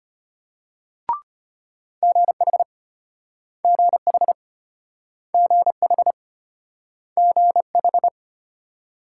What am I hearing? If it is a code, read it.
Morse code, heard as a keyed tone, about 34 wpm, 700 Hz. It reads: GH GH GH GH